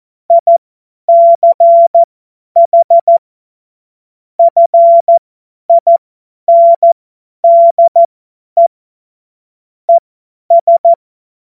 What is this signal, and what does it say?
Morse code, heard as a keyed tone, about 14 wpm, spaced as for 10 wpm, 680 Hz. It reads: ICH FINDE ES
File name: Morsekurs / Witze / Witz 31. August 2021, 50 F70